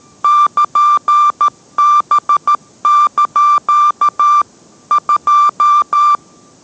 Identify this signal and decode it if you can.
Morse code, heard as a keyed tone, about 13 wpm, 1180 Hz. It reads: <KN>B) 2